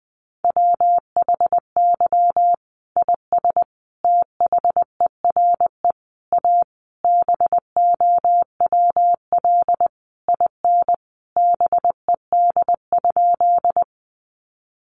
Morse code, heard as a keyed tone, about 20 wpm, 700 Hz. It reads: WHY IS THERE A BOWL IN BED?
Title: Sentences from Top 1200 Words 20wpm